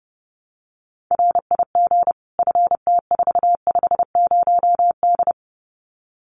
Morse code, heard as a keyed tone, about 30 wpm, 700 Hz. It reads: RIG FT450D
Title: Common QSO Phrases 30wpm